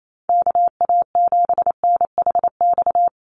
Morse code, heard as a keyed tone, about 28 wpm, 700 Hz. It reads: KA7NHX
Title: USA Call Signs 28wpm